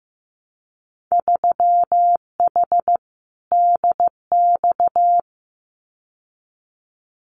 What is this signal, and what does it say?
Morse code, heard as a keyed tone, about 15 wpm, 700 Hz. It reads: 3H DX
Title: Winter Field Day Exchanges Encore 15wpm